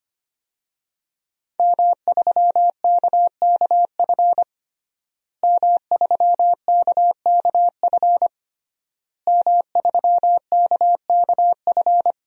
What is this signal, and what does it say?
Morse code, heard as a keyed tone, about 25 wpm, 700 Hz. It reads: M3KKF M3KKF M3KKF
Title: DXCC-100 Call Signs - 5 Characters - Repeated 3x - 25wpm